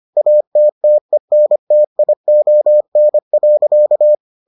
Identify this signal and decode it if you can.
Morse code, heard as a keyed tone, about 25 wpm, 600 Hz. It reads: ATTENTION.